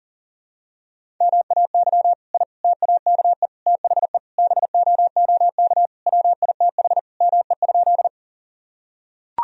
Morse code, heard as a keyed tone, about 40 wpm, 700 Hz. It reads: MAY I TAKE THE BOOK WITH ME?
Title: Sentences from Top 1000 Words - Review Part 1 - 40wpm